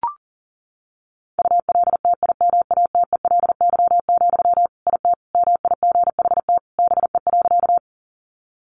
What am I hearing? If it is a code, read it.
Morse code, heard as a keyed tone, about 40 wpm, 700 Hz. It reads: ULTIMATELY, IT MIGHT BE.
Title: Sentences from Top 1200 Words 40wpm